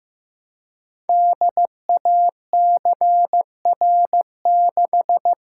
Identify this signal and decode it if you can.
Morse code, heard as a keyed tone, about 15 wpm, 700 Hz. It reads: DACR6
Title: Code Groups - 5 Letter:Numbers - 15wpm